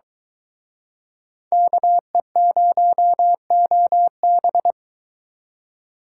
Morse code, heard as a keyed tone, about 24 wpm, 700 Hz. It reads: KE0OB